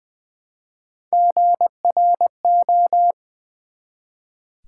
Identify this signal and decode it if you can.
Morse code, heard as a keyed tone, about 20 wpm, 700 Hz. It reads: GRO
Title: Build A Word - 1 Letter At A Time - 4-Letter Words - 20wpm